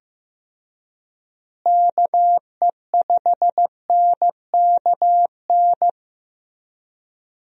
Morse code, heard as a keyed tone, about 15 wpm, 700 Hz. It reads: KE5NKN
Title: USA Call Signs - 6 Characters - Repeated 3x - 15wpm